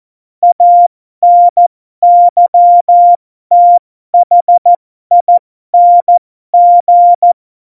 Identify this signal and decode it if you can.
Morse code, heard as a keyed tone, about 14 wpm, spaced as for 12 wpm, 700 Hz. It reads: ANYTHING
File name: Morsekurs / Witze